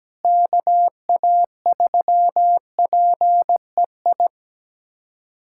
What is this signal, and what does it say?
Morse code, heard as a keyed tone, about 17 wpm, 700 Hz. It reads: KA3PEI
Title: USA Callsign DE Callsign - 6 Characters - 17wpm